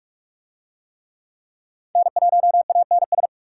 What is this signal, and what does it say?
Morse code, heard as a keyed tone, about 41 wpm, 700 Hz. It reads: N1ANS